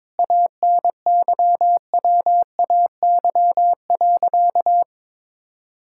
Morse code, heard as a keyed tone, about 22 wpm, 700 Hz. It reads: ANYWAY.
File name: practice / 22wpm / Sentences with 4-Word Phrases - 22wpm